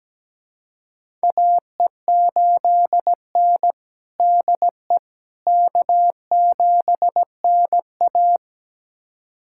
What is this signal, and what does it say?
Morse code, heard as a keyed tone, about 17 wpm, 700 Hz. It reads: AE8N DE K7NA
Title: USA Callsign DE Callsign - 4 Characters - 17wpm